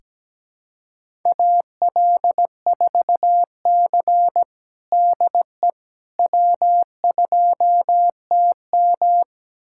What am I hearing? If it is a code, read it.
Morse code, heard as a keyed tone, about 17 wpm, 700 Hz. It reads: AL4C DE W2TM
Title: USA Callsign DE Callsign - 4 Characters - 17wpm